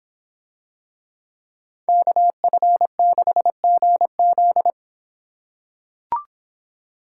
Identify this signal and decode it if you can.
Morse code, heard as a keyed tone, about 25 wpm, 700 Hz. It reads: KF6GZ